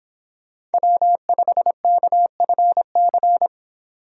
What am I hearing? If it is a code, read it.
Morse code, heard as a keyed tone, about 30 wpm, 700 Hz. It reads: W5KFC